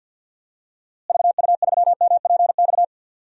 Morse code, heard as a keyed tone, about 47 wpm, 700 Hz. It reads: VU3MPX
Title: DXCC-100 Call Signs - 6 Characters - Repeated 2x - 50wpm